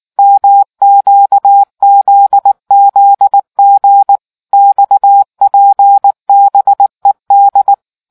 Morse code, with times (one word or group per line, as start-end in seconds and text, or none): 0.2-4.2 MQZZG
4.5-7.7 XPBED